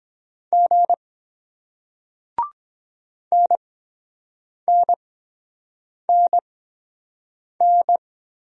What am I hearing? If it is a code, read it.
Morse code, heard as a keyed tone, about 23 wpm, spaced as for 7 wpm, 700 Hz. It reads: G NNNN